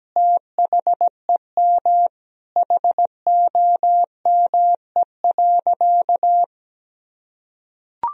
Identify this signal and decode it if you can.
Morse code, heard as a keyed tone, about 17 wpm, 700 Hz. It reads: THEM HOME.
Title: Sentences from Top 1000 Words - Review Part 3 - 17wpm